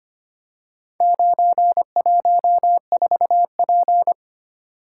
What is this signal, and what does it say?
Morse code, heard as a keyed tone, about 25 wpm, 700 Hz. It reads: 914P
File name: practice / 25wpm / Four Letter-Number Combinations - 25wpm